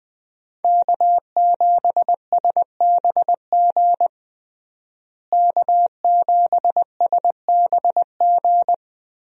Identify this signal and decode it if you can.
Morse code, heard as a keyed tone, about 20 wpm, 700 Hz. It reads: K7SBG K7SBG
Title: USA Call Signs - 5 Characters - Repeated 2x - 20wpm